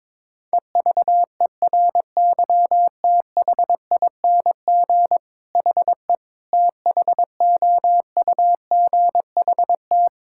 Morse code, heard as a keyed tone, about 22 wpm, 700 Hz. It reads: EVERYTHING HE THOUGHT
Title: Sentences from Top 1000 Words - Review Part 6 - 22wpm